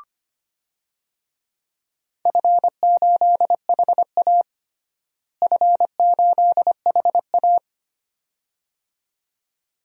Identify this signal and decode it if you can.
Morse code, heard as a keyed tone, about 25 wpm, 700 Hz. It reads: F8HA F8HA
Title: DXCC-100 Call Signs - 4 Characters - Repeated 2x - 25wpm